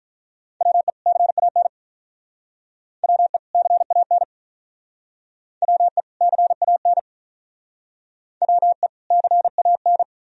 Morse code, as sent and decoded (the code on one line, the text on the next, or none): .-- . / -.-. .- -. / .-- . / -.-. .- -. / .-- . / -.-. .- -. / .-- . / -.-. .- -.
WE CAN WE CAN WE CAN WE CAN